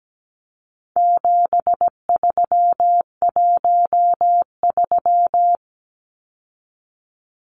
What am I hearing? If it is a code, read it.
Morse code, heard as a keyed tone, about 17 wpm, 700 Hz. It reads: 7313
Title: Numbers - 4-Digit 17wpm